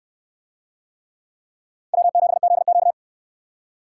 Morse code, heard as a keyed tone, about 63 wpm, 700 Hz. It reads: W7ZQ